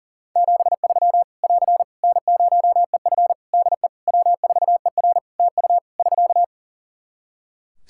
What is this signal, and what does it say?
Morse code, heard as a keyed tone, about 37 wpm, 700 Hz. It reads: 73 <AR> N0EF DE W4ER TU <SK>
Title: Practice QSO - Exchange 4 - Ending - 40wpm